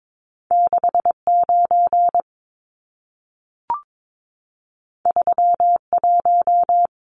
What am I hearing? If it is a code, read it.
Morse code, heard as a keyed tone, about 22 wpm, 700 Hz. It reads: 69 31
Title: Numbers - 2-Digit 22wpm